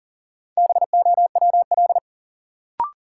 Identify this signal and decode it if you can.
Morse code, heard as a keyed tone, about 40 wpm, 700 Hz. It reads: BOWL
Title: Sentences from Top 1200 Words - New Word Drill Alt - 40wpm